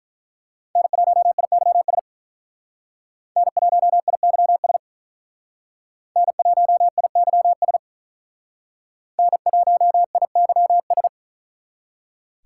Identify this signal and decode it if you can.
Morse code, heard as a keyed tone, about 43 wpm, 700 Hz. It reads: N1IYS N1IYS N1IYS N1IYS
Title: Speed Racing - USA Call Signs 35wpm